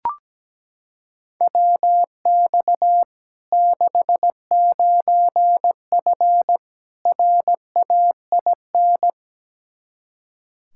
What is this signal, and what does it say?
Morse code, heard as a keyed tone, about 17 wpm, 700 Hz. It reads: WX 69F RAIN
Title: Practice QSO - Exchange 2 - Info - 17wpm